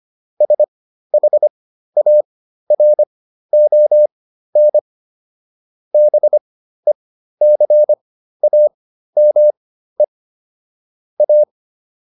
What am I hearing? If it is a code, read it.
Morse code, heard as a keyed tone, about 25 wpm, spaced as for 13 wpm, 600 Hz. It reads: SHARON BECAME A